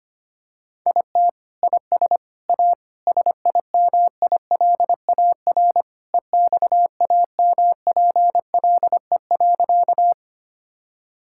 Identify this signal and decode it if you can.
Morse code, heard as a keyed tone, about 25 wpm, 700 Hz. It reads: IT IS A SIMILAR EXAMPLE.
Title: Sentences from Top 1000 Words - Review Part 5 - 25wpm